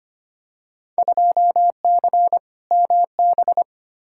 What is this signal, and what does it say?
Morse code, heard as a keyed tone, about 25 wpm, 700 Hz. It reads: 2C MB
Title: Field Day Exchanges Encore - 25wpm